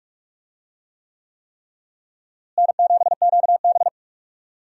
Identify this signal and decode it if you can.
Morse code, heard as a keyed tone, about 46 wpm, 700 Hz. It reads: N7QB
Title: USA Call Signs - 4 Characters - Repeated 3x - 45wpm